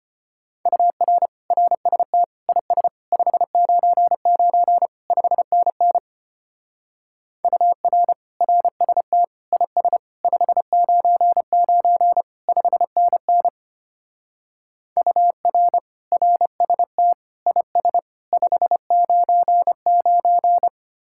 Morse code, heard as a keyed tone, about 31 wpm, 700 Hz. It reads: UR RST IS 599 5NN UR RST IS 599 5NN UR RST IS 599
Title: Speed-Racing Common QSO Phrases - 25wpm